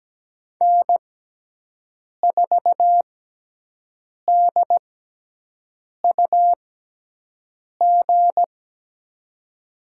Morse code, heard as a keyed tone, about 17 wpm, spaced as for 6 wpm, 700 Hz. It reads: N4DUG
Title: USA Call Signs - 5 Characters - 6x Character Spacing - 17wpm